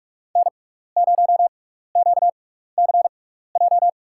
Morse code, heard as a keyed tone, about 46 wpm, spaced as for 16 wpm, 700 Hz. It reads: N0QCJ
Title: USA Call Signs - 6x Character Spacing - 45wpm